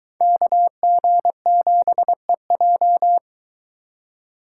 Morse code, heard as a keyed tone, about 24 wpm, 700 Hz. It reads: KG7EJ